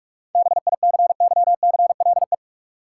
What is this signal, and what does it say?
Morse code, heard as a keyed tone, about 42 wpm, 700 Hz. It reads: BICYCLE
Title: Seven Letter Words - 45wpm